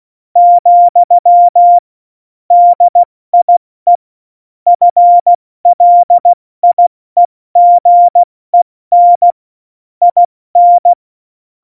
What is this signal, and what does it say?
Morse code, heard as a keyed tone, about 16 wpm, spaced as for 14 wpm, 700 Hz. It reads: , DIE FLIEGEN IN
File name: Morsekurs / Witze / MP3, Witz 08. Mai 2018, 70 F80